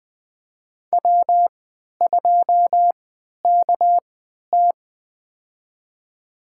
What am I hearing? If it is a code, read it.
Morse code, heard as a keyed tone, about 20 wpm, spaced as for 11 wpm, 700 Hz. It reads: W2KT
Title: USA Call Signs - 4 Characters - 3x Character Spacing - 20wpm